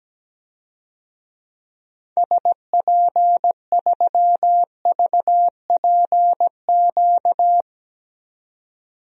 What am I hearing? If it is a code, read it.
Morse code, heard as a keyed tone, about 17 wpm, 700 Hz. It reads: SP3VPQ
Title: DXCC-100 Call Signs - 6 Characters - 17wpm